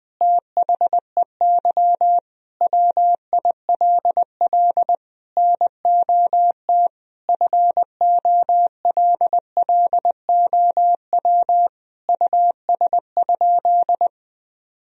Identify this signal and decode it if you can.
Morse code, heard as a keyed tone, about 20 wpm, 700 Hz. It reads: THEY WILL NOT FOLLOW US?
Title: Sentences from Top 1000 Words - Review Part 1 - 20wpm